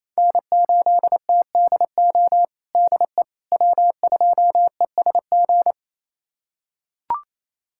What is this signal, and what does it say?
Morse code, heard as a keyed tone, about 27 wpm, 700 Hz. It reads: N8TDO DE W2ESG
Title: USA Callsign DE Callsign - 5 Characters - 28wpm